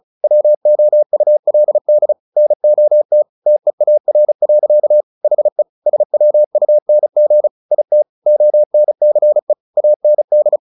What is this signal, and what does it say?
Morse code, heard as a keyed tone, about 34 wpm, 600 Hz. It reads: WOULD NOT TEAR. HE SWUNG IT ONCE AND
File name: the old man and the sea 2 35wpm